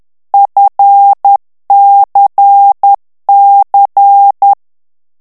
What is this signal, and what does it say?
Morse code, heard as a keyed tone, about 10 wpm, 800 Hz. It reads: FCC
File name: ANGLAIS texte clair 3 vitesse 10 mots minute